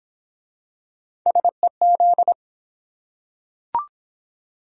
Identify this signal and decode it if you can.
Morse code, heard as a keyed tone, about 24 wpm, 700 Hz. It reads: SEZ